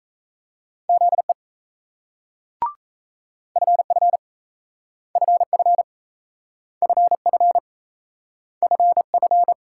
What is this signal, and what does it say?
Morse code, heard as a keyed tone, about 35 wpm, 700 Hz. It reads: ZE FF FF FF FF
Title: Speed-Racing Multi-letter Phonemes - 28wpm